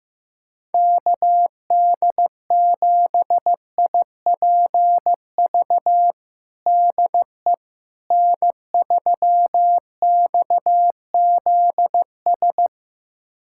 Morse code, heard as a keyed tone, about 15 wpm, 700 Hz. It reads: KD7IPV DE N3XZS
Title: USA Callsign DE Callsign 15wpm